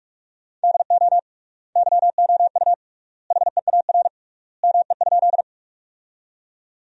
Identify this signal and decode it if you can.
Morse code, heard as a keyed tone, about 48 wpm, 700 Hz. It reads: DO YOU HEAR ME?
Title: 3x Word-Spacing - Sentences from Top 300 Words - 45wpm